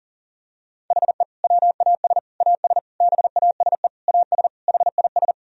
Morse code, heard as a keyed tone, about 37 wpm, 700 Hz. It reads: HE WAS AS BASE AS HIS
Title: Sentences from Top 1000 Words - Review Part 1 - 40wpm